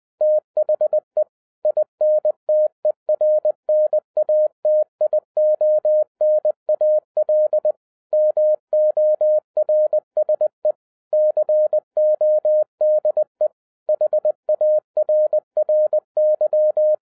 Morse code, heard as a keyed tone, about 20 wpm, 610 Hz. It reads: THE INTERNATIONAL MORSE CODE HARRY